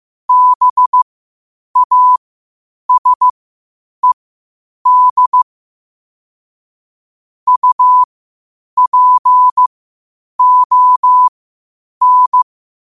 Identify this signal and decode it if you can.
Morse code, heard as a keyed tone, about 15 wpm, spaced as for 8 wpm, 1000 Hz. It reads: BASED UPON
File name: qotdfiles / 8wpm